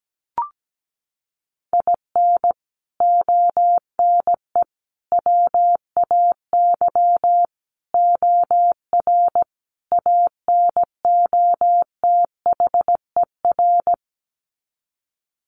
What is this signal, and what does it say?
Morse code, heard as a keyed tone, about 17 wpm, 700 Hz. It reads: IN ONE WAY OR ANOTHER
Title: Sets of 5 Words Encore 17wpm